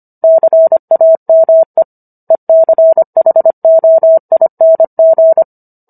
Morse code, heard as a keyed tone, about 25 wpm, 650 Hz. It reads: CAME ECHOING